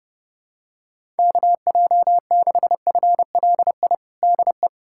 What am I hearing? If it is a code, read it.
Morse code, heard as a keyed tone, about 30 wpm, 700 Hz. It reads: KJ6FLI DE
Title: USA Callsign DE Callsign - 6 Characters - 30wpm